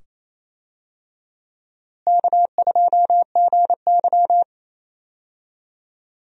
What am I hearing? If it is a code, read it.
Morse code, heard as a keyed tone, about 28 wpm, 700 Hz. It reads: K2GY
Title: USA Call Signs - 4 Characters - Repeated 2x - 28wpm